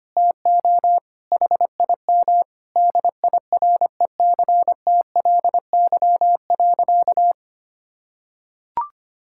Morse code, heard as a keyed tone, about 25 wpm, 700 Hz. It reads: TO HIM DIRECTLY.
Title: Sentences from Top 1000 Words - Review Part 5 - 25wpm